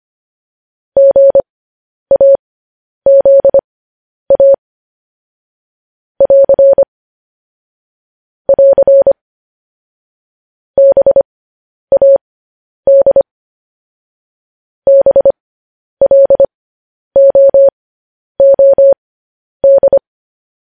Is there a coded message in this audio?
Yes: Morse code from 1.0 to 20.0 s, heard as a keyed tone, about 25 wpm, spaced as for 10 wpm, 550 Hz. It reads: GAZA <AR> <AR> BAD BLOOD